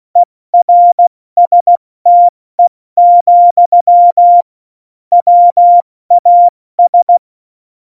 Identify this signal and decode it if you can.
Morse code, heard as a keyed tone, about 16 wpm, 700 Hz. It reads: ERSTE, WAS
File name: Morsekurs / Witze / Witz 18. Juni 2019, 70 F80